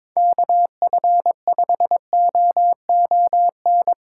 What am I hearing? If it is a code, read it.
Morse code, heard as a keyed tone, about 23 wpm, 700 Hz. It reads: KF5OON
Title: Speed Racing - USA 2x3 Call Signs 22wpm